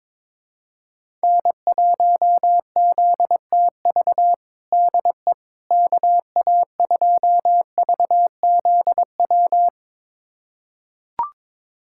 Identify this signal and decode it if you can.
Morse code, heard as a keyed tone, about 22 wpm, 700 Hz. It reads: N1ZTV DE KA2VZW
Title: USA Callsign DE Callsign 22wpm